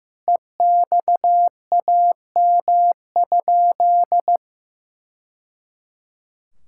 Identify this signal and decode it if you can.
Morse code, heard as a keyed tone, about 15 wpm, 700 Hz. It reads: EXAM?
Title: Sentences from Top 1000 Words - Review Part 5 - 15wpm